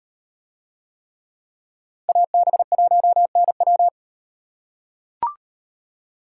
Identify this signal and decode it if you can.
Morse code, heard as a keyed tone, about 36 wpm, 700 Hz. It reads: AB1NW